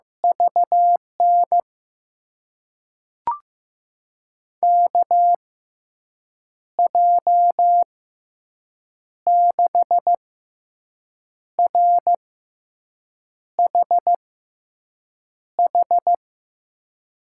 Morse code, heard as a keyed tone, about 15 wpm, spaced as for 6 wpm, 700 Hz. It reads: VN KJ6RHH
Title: USA Call Signs - 6 Characters - 6x Character Spacing - 15wpm